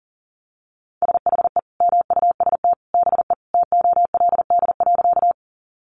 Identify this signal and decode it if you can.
Morse code, heard as a keyed tone, about 40 wpm, 700 Hz. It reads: SHE MUST BE TOLD.
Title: Sentences from Top 400 Words 40wpm